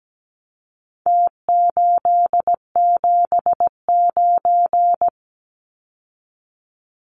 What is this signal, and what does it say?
Morse code, heard as a keyed tone, about 17 wpm, 700 Hz. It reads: T879
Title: Numbers - Contest Style - 4-Digit 17wpm